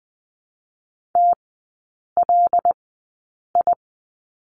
Morse code, heard as a keyed tone, about 20 wpm, spaced as for 8 wpm, 700 Hz. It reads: TLI
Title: 087 Lesson 15e - Number 5 - Three Characters 20wpm